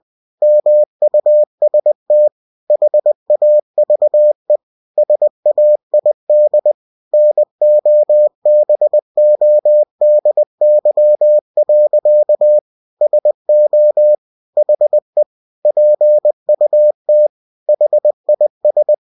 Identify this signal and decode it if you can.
Morse code, heard as a keyed tone, about 20 wpm, 600 Hz. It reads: MUST HAVE SAID NOBODY. SO HE PUT HIS
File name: cw-books / winnie-the-pooh / winnie-20wpm-page-0014